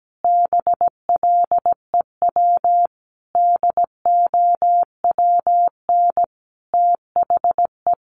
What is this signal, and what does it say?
Morse code, heard as a keyed tone, about 17 wpm, 700 Hz. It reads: BLEW DOWN THE
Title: A Sound Of Thunder 17wpm